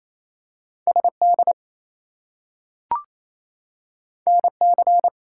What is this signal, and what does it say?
Morse code, heard as a keyed tone, about 28 wpm, 700 Hz. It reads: SD NC
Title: US States and Canadian Provinces 28wpm